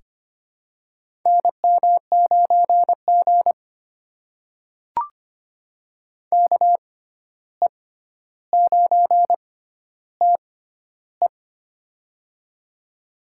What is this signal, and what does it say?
Morse code, heard as a keyed tone, about 25 wpm, spaced as for 12 wpm, 700 Hz. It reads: NM9G KE9TE
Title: USA Call Signs - 6x Character Spacing - 25wpm